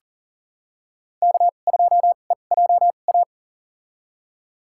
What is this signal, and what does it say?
Morse code, heard as a keyed tone, about 37 wpm, spaced as for 28 wpm, 700 Hz. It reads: K2EJA
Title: USA Call Signs - 5 Characters - 2x Character Spacing - 40wpm